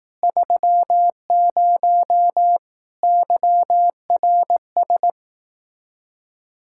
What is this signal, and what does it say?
Morse code, heard as a keyed tone, about 18 wpm, 700 Hz. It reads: 30 YRS
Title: Speed-Racing Common QSO Phrases Encore - 15wpm